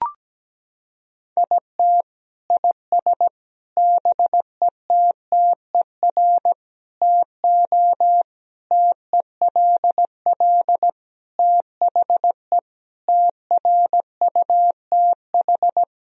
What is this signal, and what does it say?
Morse code, heard as a keyed tone, about 17 wpm, 700 Hz. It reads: IT IS BETTER TO TELL THE TRUTH
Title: Sentences from Top 1000 Words - Review Part 4 - 17wpm